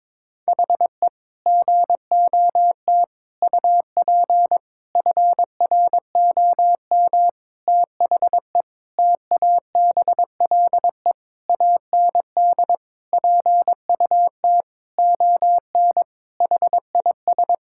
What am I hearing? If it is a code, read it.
Morse code, heard as a keyed tone, about 22 wpm, 700 Hz. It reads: HE GOT UP FROM THE TABLE AND PUT ON HIS
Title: Sentences from Top 1600 Words 22wpm